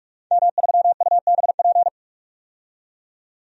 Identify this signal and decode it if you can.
Morse code, heard as a keyed tone, about 47 wpm, 700 Hz. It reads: M3UBP
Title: DXCC-100 Call Signs - 5 Characters - 45wpm